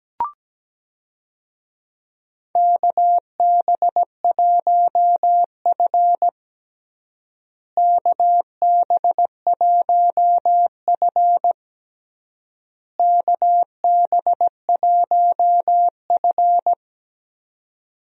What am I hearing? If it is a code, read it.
Morse code, heard as a keyed tone, about 17 wpm, 700 Hz. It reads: KB1F KB1F KB1F
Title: USA Call Signs - 4 Characters - Repeated 3x - 17wpm